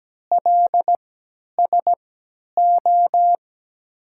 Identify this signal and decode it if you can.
Morse code, heard as a keyed tone, about 17 wpm, 700 Hz. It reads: L S O
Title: 3x Character Spacing - Top 500 Words - 17wpm